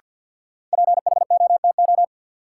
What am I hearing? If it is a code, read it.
Morse code, heard as a keyed tone, about 52 wpm, 700 Hz. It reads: PHOTO
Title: Sentences from Top 1200 Words - New Word Drill - 50wpm